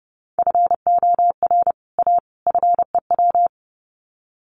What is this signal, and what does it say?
Morse code, heard as a keyed tone, about 29 wpm, 700 Hz. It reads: FOR A FEW
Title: Sets of 3 Words 30wpm